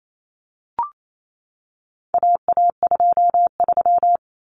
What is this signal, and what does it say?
Morse code, heard as a keyed tone, about 28 wpm, 700 Hz. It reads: AA23
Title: Common Cut Numbers - 4-Digit 28wpm